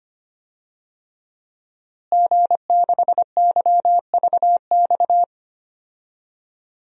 Morse code, heard as a keyed tone, about 25 wpm, 700 Hz. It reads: G6YVX
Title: DXCC-100 Call Signs - 5 Characters - Repeated 3x - 25wpm